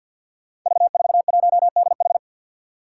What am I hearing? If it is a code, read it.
Morse code, heard as a keyed tone, about 53 wpm, 700 Hz. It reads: V41DH
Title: Five Challenging Letter-Number Combinations 50wpm